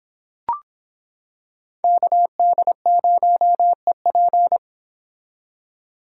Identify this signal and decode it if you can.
Morse code, heard as a keyed tone, about 26 wpm, 700 Hz. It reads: KD0EP